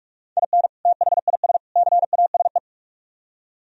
Morse code, heard as a keyed tone, about 44 wpm, 700 Hz. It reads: IN THIS CASE